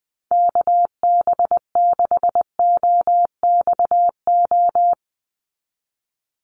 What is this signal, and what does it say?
Morse code, heard as a keyed tone, about 20 wpm, 700 Hz. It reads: KB6OXO
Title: USA Call Signs 20wpm